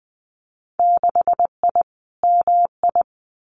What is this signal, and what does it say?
Morse code, heard as a keyed tone, about 20 wpm, 700 Hz. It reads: 6I MI
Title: Winter Field Day Exchanges Encore 20wpm